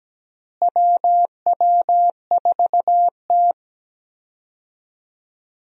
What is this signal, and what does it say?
Morse code, heard as a keyed tone, about 17 wpm, 700 Hz. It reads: WW4T
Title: USA Call Signs - 4 Characters - Repeated 3x - 17wpm